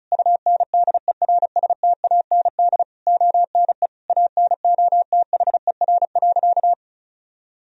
Morse code, heard as a keyed tone, about 34 wpm, 700 Hz. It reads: UNDERSTAND ONE ANOTHER.
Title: Sentences from Top 1000 Words - Review Part 1 - 35wpm